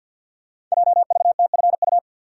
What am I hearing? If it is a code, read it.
Morse code, heard as a keyed tone, about 52 wpm, 700 Hz. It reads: JVTFU
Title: Code Groups - 5 Letter:Numbers - 50wpm